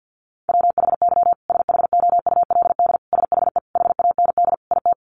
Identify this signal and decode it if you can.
Morse code, heard as a keyed tone, about 49 wpm, 700 Hz. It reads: WHY SHOULD SHE HAND IT
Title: Sentences from Top 900 Words 50wpm